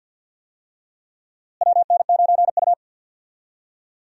Morse code, heard as a keyed tone, about 49 wpm, 700 Hz. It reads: WN9U
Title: USA Call Signs - 4 Characters - Repeated 2x - 50wpm